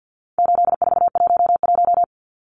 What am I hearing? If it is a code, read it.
Morse code, heard as a keyed tone, about 52 wpm, 700 Hz. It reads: 8411
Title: Numbers - 4-Digit 50wpm